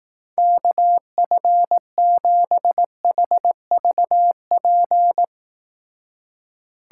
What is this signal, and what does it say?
Morse code, heard as a keyed tone, about 18 wpm, 700 Hz. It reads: KF7HVP
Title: Speed Racing - USA 2x3 Call Signs 15wpm